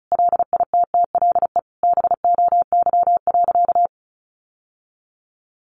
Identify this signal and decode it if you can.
Morse code, heard as a keyed tone, about 34 wpm, 700 Hz. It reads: LITTLE BOY.
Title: Sentences from Top 1000 Words 35wpm